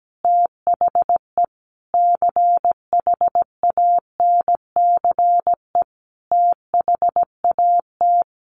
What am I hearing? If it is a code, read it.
Morse code, heard as a keyed tone, about 17 wpm, 700 Hz. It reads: THE CHANCE THAT
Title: Sentences from Top 800 Words 17wpm